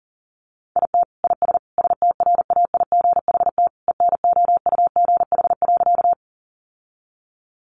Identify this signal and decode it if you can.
Morse code, heard as a keyed tone, about 40 wpm, 700 Hz. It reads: IT IS STRAIGHT ENOUGH.
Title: Sentences from Top 1000 Words 40wpm